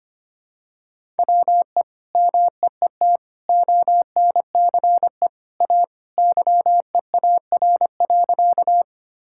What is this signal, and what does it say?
Morse code, heard as a keyed tone, about 25 wpm, 700 Hz. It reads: WE MEET ONCE A YEAR.
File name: practice / 25wpm / Sentences with 2-Word Phrases - 25wpm